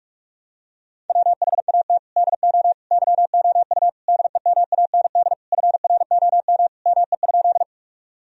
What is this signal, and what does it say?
Morse code, heard as a keyed tone, about 44 wpm, 700 Hz. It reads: WHAT DO YOU DEMAND FROM ME?
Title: Sentences from Top 1000 Words - Review Part 4 - 45wpm